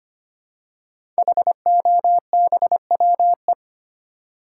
Morse code, heard as a keyed tone, about 25 wpm, 700 Hz. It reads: HOBWE
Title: Code Groups - 5 Letters - 25wpm